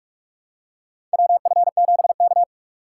Morse code, heard as a keyed tone, about 44 wpm, 700 Hz. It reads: WF7K
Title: USA Call Signs - 4 Characters - 6x Character Spacing - 45wpm